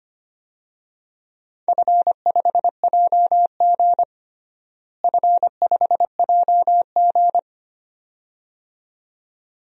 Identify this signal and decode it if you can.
Morse code, heard as a keyed tone, about 25 wpm, 700 Hz. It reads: F5JG F5JG
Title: DXCC-100 Call Signs - 4 Characters - Repeated 2x - 25wpm